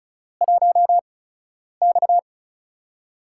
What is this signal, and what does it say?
Morse code, heard as a keyed tone, about 34 wpm, spaced as for 10 wpm, 700 Hz. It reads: 1X
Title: USA Call Signs - 8x Character Spacing - 35wpm